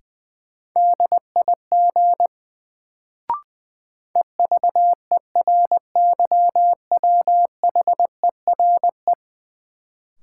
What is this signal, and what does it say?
Morse code, heard as a keyed tone, about 20 wpm, 700 Hz. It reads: DIG EVERYWHERE
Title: Sentences from Top 1600 Words - New Word Drill - 20wpm